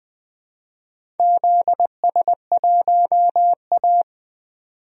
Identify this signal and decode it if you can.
Morse code, heard as a keyed tone, about 20 wpm, 700 Hz. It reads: ZS1A
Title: DXCC-100 Call Signs - 4 Characters - 20wpm